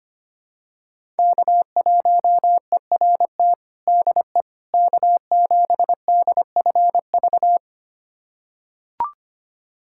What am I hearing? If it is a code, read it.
Morse code, heard as a keyed tone, about 25 wpm, 700 Hz. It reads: K1ERT DE K7DFV
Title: USA Callsign DE Callsign - 5 Characters - 25wpm